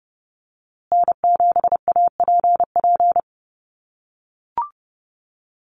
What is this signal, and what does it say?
Morse code, heard as a keyed tone, about 30 wpm, 700 Hz. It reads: N7APP
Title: USA Call Signs Repeated 2x 30wpm